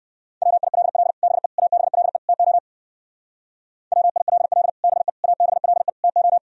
Morse code, heard as a keyed tone, about 62 wpm, 700 Hz. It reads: WILL BE ABLE TO WILL BE ABLE TO